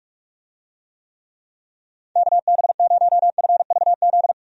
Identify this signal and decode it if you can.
Morse code, heard as a keyed tone, about 46 wpm, 700 Hz. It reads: KB0FVZ
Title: USA Call Signs - 6 Characters - Repeated 3x - 45wpm